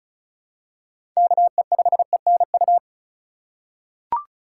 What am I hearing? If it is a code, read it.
Morse code, heard as a keyed tone, about 36 wpm, 700 Hz. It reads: KE5ENU
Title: USA Call Signs - 6 Characters - 3x Character Spacing - 35wpm